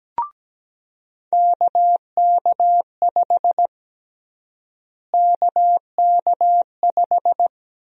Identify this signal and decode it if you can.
Morse code, heard as a keyed tone, about 17 wpm, 700 Hz. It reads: KK5 KK5
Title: USA Call Sign Prefixes - Repeated 2x - 17wpm